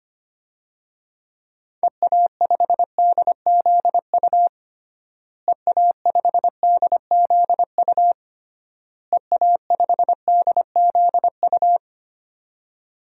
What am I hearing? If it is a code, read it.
Morse code, heard as a keyed tone, about 25 wpm, 700 Hz. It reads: EA5DZU EA5DZU EA5DZU
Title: DXCC-100 Call Signs - 6 Characters - Repeated 3x - 25wpm